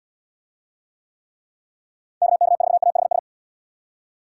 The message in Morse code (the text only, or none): KK4AHS